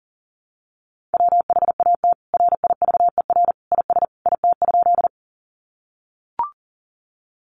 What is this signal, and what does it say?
Morse code, heard as a keyed tone, about 40 wpm, 700 Hz. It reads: WHAT RIVER IS IT?
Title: Sentences from Top 300 Words 40wpm